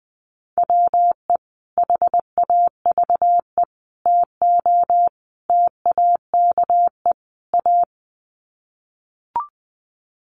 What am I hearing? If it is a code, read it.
Morse code, heard as a keyed tone, about 20 wpm, 700 Hz. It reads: WE HAVE TO TAKE A
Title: Sets of 5 Words Encore 20wpm